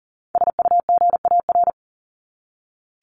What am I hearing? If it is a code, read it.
Morse code, heard as a keyed tone, about 40 wpm, 700 Hz. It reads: SUGAR
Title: Top 1001-1250 Words 40wpm